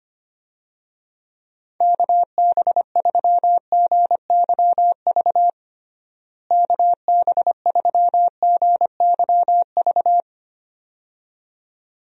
Morse code, heard as a keyed tone, about 25 wpm, 700 Hz. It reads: KB3GYV KB3GYV
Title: USA Call Signs - 6 Characters - Repeated 2x - 25wpm